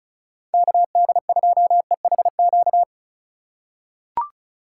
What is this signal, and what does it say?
Morse code, heard as a keyed tone, about 36 wpm, 700 Hz. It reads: KD2EHQ
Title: USA Call Signs - 6 Characters - 2x Character Spacing - 35wpm